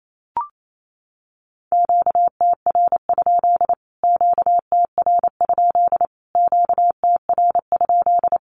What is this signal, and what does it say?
Morse code, heard as a keyed tone, about 28 wpm, 700 Hz. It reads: QTR? QTR? QTR?
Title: Q-Codes Repeated 3x 28wpm